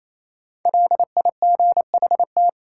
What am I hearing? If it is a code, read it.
Morse code, heard as a keyed tone, about 28 wpm, 700 Hz. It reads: LIGHT
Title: Sentences from Top 200 Words - New Word Drill - 28wpm